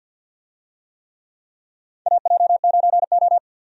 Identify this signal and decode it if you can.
Morse code, heard as a keyed tone, about 53 wpm, 700 Hz. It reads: AJ9O
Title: USA Call Signs - 4 Characters - 50wpm